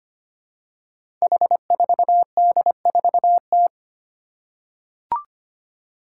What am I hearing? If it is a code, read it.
Morse code, heard as a keyed tone, about 25 wpm, 700 Hz. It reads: H4D4T
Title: Five Challenging Letter-Number Combinations 25wpm